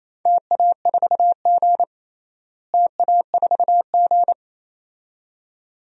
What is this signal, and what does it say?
Morse code, heard as a keyed tone, about 28 wpm, 700 Hz. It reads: TA4G TA4G
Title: DXCC-100 Call Signs - 4 Characters - Repeated 2x - 28wpm